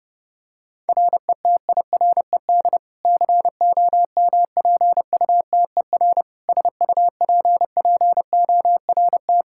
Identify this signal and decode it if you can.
Morse code, heard as a keyed tone, about 30 wpm, 700 Hz. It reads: RETIRED COMPUTER SUPPORT
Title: Practice QSO - Exchange 3 - More Info - 30wpm